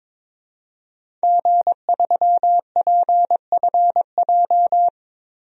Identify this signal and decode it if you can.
Morse code, heard as a keyed tone, about 22 wpm, 700 Hz. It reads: G3PFJ
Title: DXCC-100 Call Signs - 5 Characters - Repeated 2x - 22wpm